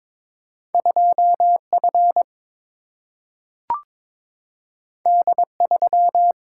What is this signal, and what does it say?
Morse code, heard as a keyed tone, about 22 wpm, 700 Hz. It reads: 2 F D 3
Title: Single-Letter Single-Number 22wpm